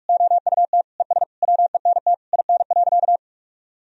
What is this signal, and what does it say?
Morse code, heard as a keyed tone, about 46 wpm, 700 Hz. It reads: OUT ES WENT IN.
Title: Sentences from Top 300 Words and Abbreviations 45wpm